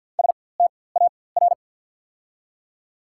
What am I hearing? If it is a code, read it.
Morse code, heard as a keyed tone, about 54 wpm, spaced as for 24 wpm, 700 Hz. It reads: STAR
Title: 4x Character Spacing - Top 500 Words - 50wpm